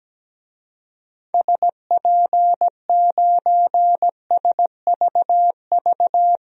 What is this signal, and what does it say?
Morse code, heard as a keyed tone, about 17 wpm, 700 Hz. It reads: SP9SVV
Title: DXCC-100 Call Signs - 6 Characters - 17wpm